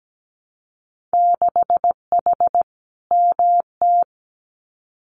Morse code, heard as a keyed tone, about 17 wpm, 700 Hz. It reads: 6H MT
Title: Winter Field Day Exchanges Encore 17wpm